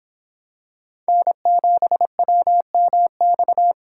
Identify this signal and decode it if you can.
Morse code, heard as a keyed tone, about 26 wpm, 700 Hz. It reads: N7WMX